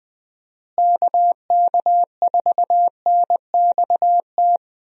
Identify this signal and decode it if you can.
Morse code, heard as a keyed tone, about 20 wpm, 700 Hz. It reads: KK4NXT
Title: USA Call Signs - 6 Characters - Repeated 2x - 20wpm